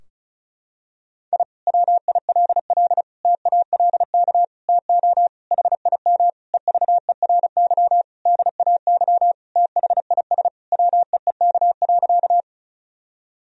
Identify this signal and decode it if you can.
Morse code, heard as a keyed tone, about 36 wpm, 700 Hz. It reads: I WILL TALK TO HIM EVERY DAY THIS WEEK.
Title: Sentences from Top 1000 Words - Review Part 2 - 35wpm